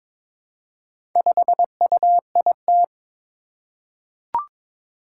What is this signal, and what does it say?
Morse code, heard as a keyed tone, about 22 wpm, 700 Hz. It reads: 5UIT
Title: Four Challenging Letter-Number Combinations 22wpm